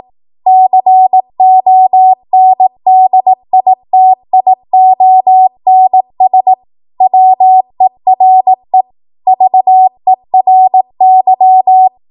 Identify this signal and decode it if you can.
Morse code, heard as a keyed tone, about 18 wpm, 750 Hz. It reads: CONDITIONS WERE VERY
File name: Archive / 18 WPM / 18WPM